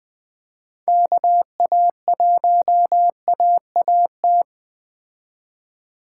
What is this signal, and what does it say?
Morse code, heard as a keyed tone, about 21 wpm, 700 Hz. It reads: KA1AAT